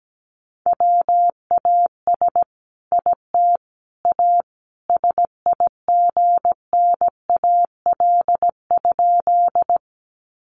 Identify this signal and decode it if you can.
Morse code, heard as a keyed tone, about 17 wpm, 700 Hz. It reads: WAS IT A SIGNAL?
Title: Sentences from Top 1400 Words 17wpm